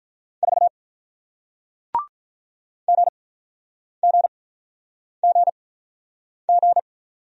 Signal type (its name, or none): Morse code